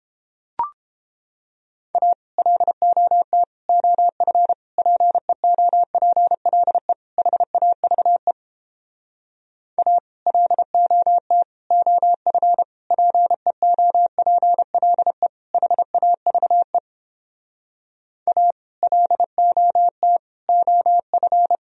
Morse code, heard as a keyed tone, about 31 wpm, 700 Hz. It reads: A LOT OF PEOPLE HAVE A LOT OF PEOPLE HAVE A LOT OF